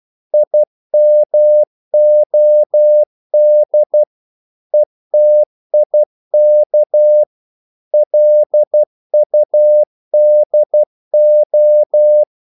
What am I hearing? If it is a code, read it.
Morse code, heard as a keyed tone, about 12 wpm, 600 Hz. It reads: IMOD ETIK LUDO
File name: Korte ord DK 12wpm